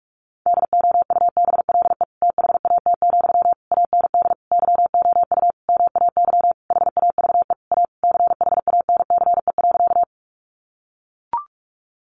Morse code, heard as a keyed tone, about 44 wpm, 700 Hz. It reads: DOUBLE THAT, AND YOU MAY HAVE A CHANCE.
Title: Sentences from Top 1400 Words 45wpm